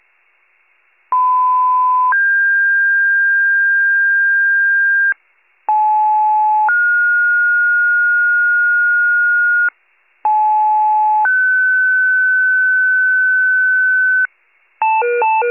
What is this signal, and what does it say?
Two-tone page, heard as a one-second tone, then a three-second tone, about 1010 then 1600 Hz, about 860 then 1430 Hz, about 860 then 1500 Hz. It is heard from 1.1 to 14.3 s.